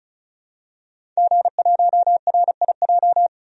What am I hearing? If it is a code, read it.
Morse code, heard as a keyed tone, about 36 wpm, 700 Hz. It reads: G1RIJ
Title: DXCC-100 Call Signs - 5 Characters - 35wpm